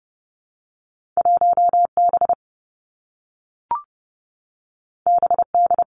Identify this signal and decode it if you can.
Morse code, heard as a keyed tone, about 30 wpm, 700 Hz. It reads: 1B BD
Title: Two Challenging Letter-Number Combinations 30wpm